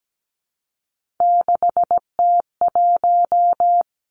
Morse code, heard as a keyed tone, about 17 wpm, 700 Hz. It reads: 6T1
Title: Three Challenging Letter-Number Combinations 17wpm